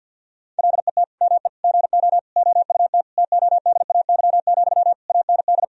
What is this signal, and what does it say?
Morse code, heard as a keyed tone, about 48 wpm, 700 Hz. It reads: LET ME GO OUT TODAY, AND
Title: Sentences from Top 1000 Words - Review Part 4 - 50wpm